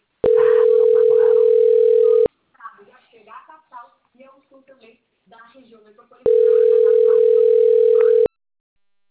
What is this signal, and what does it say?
Ringback tone (North America), 440 + 480 Hz, with speech.
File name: rg-1600-unknown- - - .4472